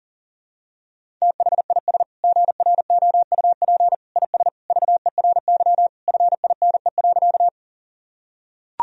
Morse code, heard as a keyed tone, about 40 wpm, 700 Hz. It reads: THIS GROUP IS VERY FINE.
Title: Sentences from Top 1000 Words - Review Part 1 - 40wpm